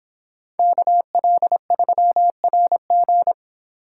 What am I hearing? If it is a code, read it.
Morse code, heard as a keyed tone, about 25 wpm, 700 Hz. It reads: KL3RG